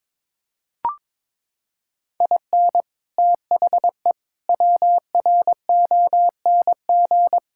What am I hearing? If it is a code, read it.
Morse code, heard as a keyed tone, about 22 wpm, 700 Hz. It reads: IN THE WRONG
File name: practice / 22wpm / Sets of 8 Words - 22wpm